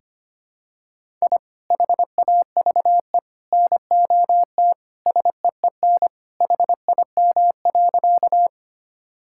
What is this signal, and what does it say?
Morse code, heard as a keyed tone, about 25 wpm, 700 Hz. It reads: I HAVE NOT SEEN HIM.
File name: practice / 25wpm / Sentences with 3-Word Phrases - 25wpm